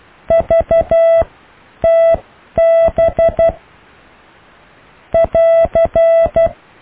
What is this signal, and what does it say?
Morse code, heard as a keyed tone, about 12 wpm, spaced as for 9 wpm, 660 Hz. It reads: VTB <AR>
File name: cw / Alle letters